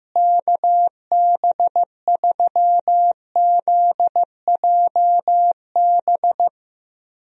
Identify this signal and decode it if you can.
Morse code, heard as a keyed tone, about 15 wpm, 700 Hz. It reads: KB3ZJB
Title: USA Call Signs - 6 Characters - Repeated 3x - 15wpm